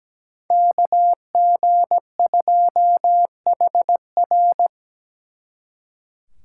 Morse code, heard as a keyed tone, about 20 wpm, 700 Hz. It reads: KG2HR
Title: Speed Racing - USA 2x2 Call Signs 17wpm